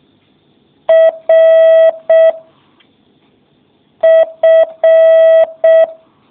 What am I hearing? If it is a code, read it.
Morse code, heard as a keyed tone, about 6 wpm, 650 Hz. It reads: R F